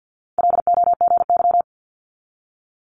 Morse code, heard as a keyed tone, about 50 wpm, 700 Hz. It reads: LOGY
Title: Multi-letter Phonemes 50wpm